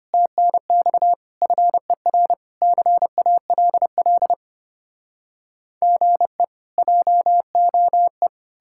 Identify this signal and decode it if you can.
Morse code, heard as a keyed tone, about 33 wpm, 700 Hz. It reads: TNX FER CALL GE JOE